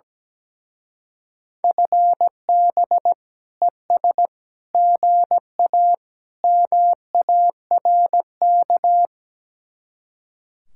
Morse code, heard as a keyed tone, about 17 wpm, 700 Hz. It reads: FB ES GA MARK
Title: Speed-Racing Common QSO Phrases - 17wpm